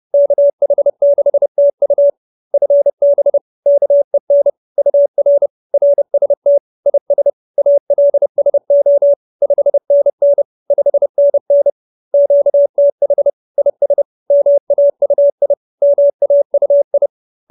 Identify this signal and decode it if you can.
Morse code, heard as a keyed tone, about 30 wpm, 570 Hz. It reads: KH6TU FB KEN UR RST IS ALSO 5NN 5NN QTH IS MAUI MAUI